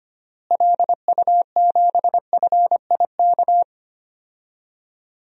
Morse code, heard as a keyed tone, about 25 wpm, 700 Hz. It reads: LU7FIK
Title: DXCC-100 Call Signs - 6 Characters - Repeated 2x - 25wpm